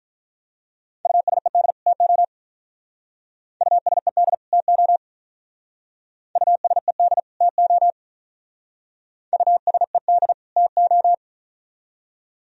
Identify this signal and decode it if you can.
Morse code, heard as a keyed tone, about 42 wpm, 700 Hz. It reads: USED TO USED TO USED TO USED TO
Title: Speed-Racing - Sets of 2 Words 35wpm